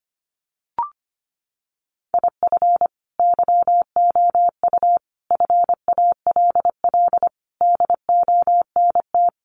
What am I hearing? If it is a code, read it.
Morse code, heard as a keyed tone, about 25 wpm, 700 Hz. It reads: IF YOU FALL DONT
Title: Sentences from Top 500 Words 25wpm